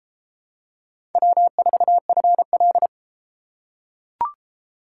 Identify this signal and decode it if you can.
Morse code, heard as a keyed tone, about 32 wpm, 700 Hz. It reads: W4FL